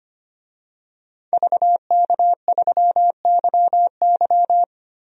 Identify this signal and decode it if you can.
Morse code, heard as a keyed tone, about 25 wpm, 700 Hz. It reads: VK3YY
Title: DXCC-100 Call Signs - 5 Characters - 25wpm